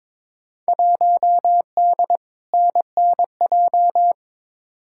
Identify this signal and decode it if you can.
Morse code, heard as a keyed tone, about 22 wpm, 700 Hz. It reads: 1D NNJ
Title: Field Day Exchanges - 22wpm